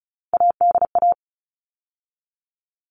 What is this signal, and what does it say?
Morse code, heard as a keyed tone, about 34 wpm, 700 Hz. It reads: ADA
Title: All Cut Numbers - 3-Digit 35wpm